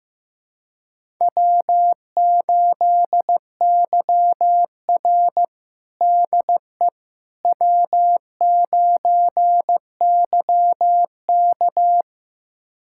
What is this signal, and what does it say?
Morse code, heard as a keyed tone, about 15 wpm, 700 Hz. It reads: W8YR DE W9YK
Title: USA Callsign DE Callsign - 4 Characters - 15wpm